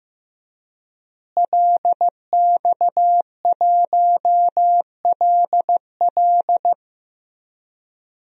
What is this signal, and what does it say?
Morse code, heard as a keyed tone, about 16 wpm, 700 Hz. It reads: LX1LL
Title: DXCC-100 Call Signs - 5 Characters - Repeated 2x - 15wpm